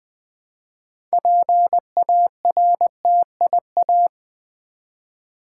Morse code, heard as a keyed tone, about 20 wpm, 700 Hz. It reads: PARTIA
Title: Build A Word - 3 Letters At A Time - 9-Letter Words - 20wpm